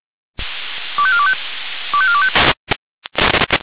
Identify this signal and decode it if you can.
Five-tone selective call (ZVEI), heard as quick runs of five tones, tones of about 70 ms. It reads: 25425 25425
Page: SDR/ZVEI